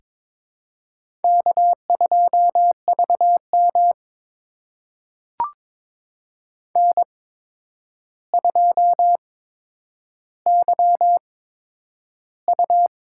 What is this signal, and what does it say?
Morse code, heard as a keyed tone, about 23 wpm, spaced as for 10 wpm, 700 Hz. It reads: K2VM N2YU
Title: USA Call Signs - 8x Character Spacing - 22wpm